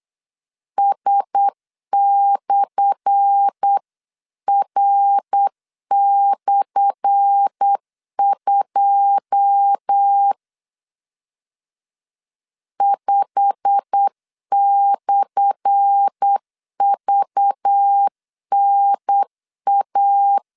Morse code, heard as a keyed tone, about 8 wpm, 800 Hz. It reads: S/R/2 5/VNA
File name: LEZIONI / 10 SIGNS / 10A SLASH 8wpm 3452